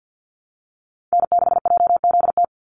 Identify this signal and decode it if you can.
Morse code, heard as a keyed tone, about 50 wpm, 700 Hz. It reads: N6JZT
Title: NA SOTA Activator Callsigns 50wpm